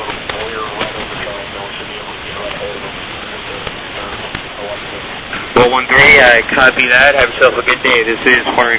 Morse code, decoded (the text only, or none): none